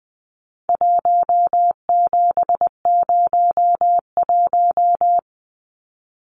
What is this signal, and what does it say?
Morse code, heard as a keyed tone, about 20 wpm, 700 Hz. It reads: 1701
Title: Numbers - 4-Digit 20wpm